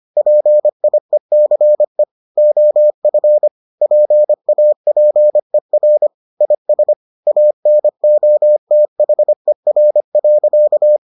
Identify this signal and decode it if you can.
Morse code, heard as a keyed tone, about 25 wpm, 600 Hz. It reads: PIECE OF PAPER IS ANOTHER.